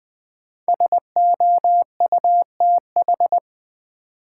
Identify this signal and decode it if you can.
Morse code, heard as a keyed tone, about 21 wpm, 700 Hz. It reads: SOUTH
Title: Sentences from Top 400 Words - New Word Drill - 20wpm